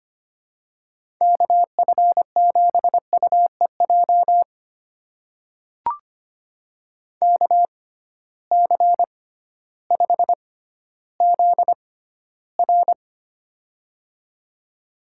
Morse code, heard as a keyed tone, about 25 wpm, spaced as for 14 wpm, 700 Hz. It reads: KF7UEJ KC5ZR
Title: USA Call Signs - 6x Character Spacing - 25wpm